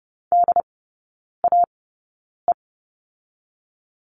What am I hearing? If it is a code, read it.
Morse code, heard as a keyed tone, about 30 wpm, spaced as for 9 wpm, 700 Hz. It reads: DAE
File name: 081 Lesson 14e - Number 2 - Three Characters 30wpm